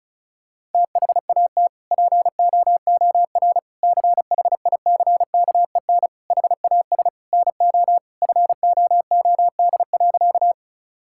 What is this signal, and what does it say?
Morse code, heard as a keyed tone, about 36 wpm, 700 Hz. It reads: THAT POOR CHICKEN HAS NO FOOD.
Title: Sentences from Top 1000 Words - Review Part 5 - 35wpm